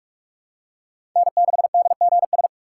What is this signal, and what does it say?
Morse code, heard as a keyed tone, about 44 wpm, 700 Hz. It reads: N6DGS
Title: USA Call Signs - 5 Characters - 6x Character Spacing - 45wpm